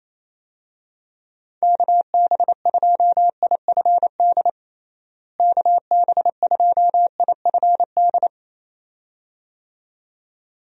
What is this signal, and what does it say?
Morse code, heard as a keyed tone, about 28 wpm, 700 Hz. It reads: KB2IFD KB2IFD
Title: USA Call Signs - 6 Characters - Repeated 2x - 28wpm